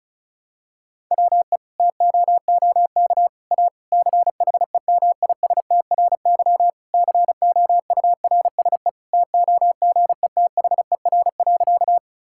Morse code, heard as a keyed tone, about 34 wpm, 700 Hz. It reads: WE TOOK A CHEMISTRY COURSE TOGETHER.
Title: Sentences from Top 1000 Words - Review Part 6 - 35wpm